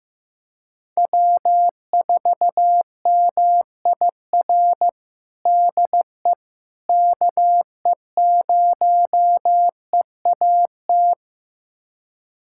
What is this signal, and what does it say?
Morse code, heard as a keyed tone, about 15 wpm, 700 Hz. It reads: W4MIR DE KE0EAT
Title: USA Callsign DE Callsign 15wpm